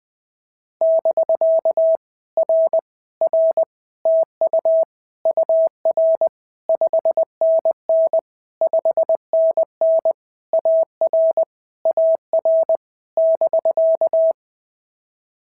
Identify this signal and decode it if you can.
Morse code, heard as a keyed tone, about 20 wpm, 650 Hz. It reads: <BK> R R TU UR 5NN 5NN AR AR <BK>
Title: POTA QSO Practice - 20wpm